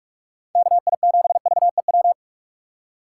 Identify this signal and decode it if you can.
Morse code, heard as a keyed tone, about 43 wpm, 700 Hz. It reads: KI7VEW
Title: USA Call Signs - 6 Characters - 3x Character Spacing - 45wpm